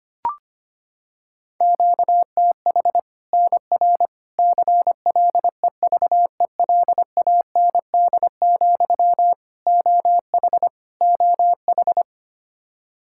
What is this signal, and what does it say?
Morse code, heard as a keyed tone, about 25 wpm, 700 Hz. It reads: QTH NR CLEVELAND, OH OH
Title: K1USN SST QSO Practice 25wpm